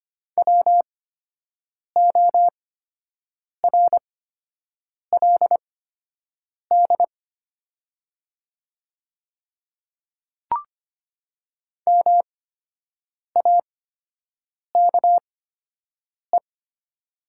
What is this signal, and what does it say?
Morse code, heard as a keyed tone, about 25 wpm, spaced as for 7 wpm, 700 Hz. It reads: WORLD MAKE
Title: 8x Character Spacing - Top 500 Words - 25wpm